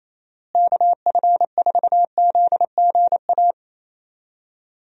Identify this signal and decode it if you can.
Morse code, heard as a keyed tone, about 28 wpm, 700 Hz. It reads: KF4ZGA
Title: USA Call Signs - 6 Characters - Repeated 3x - 28wpm